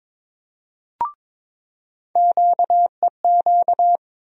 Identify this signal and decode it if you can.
Morse code, heard as a keyed tone, about 22 wpm, 700 Hz. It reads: QEQ
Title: USA Call Sign Postfixes - 3 Letters 22wpm